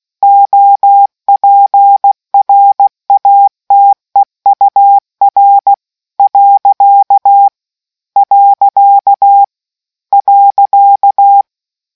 Morse code, heard as a keyed tone, about 16 wpm, 800 Hz. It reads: OPRATEUR . .